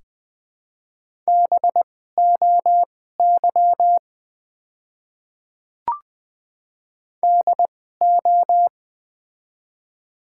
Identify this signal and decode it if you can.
Morse code, heard as a keyed tone, about 20 wpm, spaced as for 14 wpm, 700 Hz. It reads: BOY DO
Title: 2x Character Spacing - Top 200 Words - 20wpm